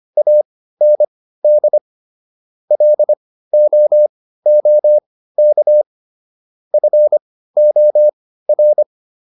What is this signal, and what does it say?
Morse code, heard as a keyed tone, about 25 wpm, spaced as for 15 wpm, 600 Hz. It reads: AND LOOK FOR